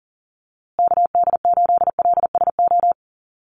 Morse code, heard as a keyed tone, about 40 wpm, 700 Hz. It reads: KD8LSO
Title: USA Call Signs Repeated 2x 40wpm